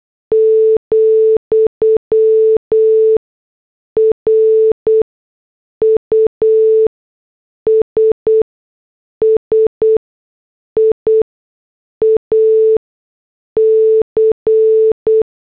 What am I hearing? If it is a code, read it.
Morse code, heard as a keyed tone, about 8 wpm, spaced as for 6 wpm, 440 Hz. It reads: , RUSSIA C
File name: morse / headlines